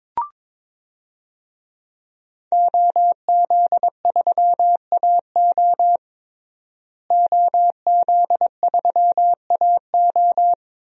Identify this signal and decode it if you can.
Morse code, heard as a keyed tone, about 22 wpm, 700 Hz. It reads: OZ3AO OZ3AO
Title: DXCC-100 Call Signs - 5 Characters - Repeated 2x - 22wpm